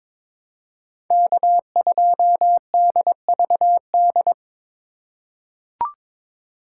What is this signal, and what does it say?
Morse code, heard as a keyed tone, about 22 wpm, 700 Hz. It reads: K2DVD
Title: USA Call Signs - 5 Characters - 3x Character Spacing - 22wpm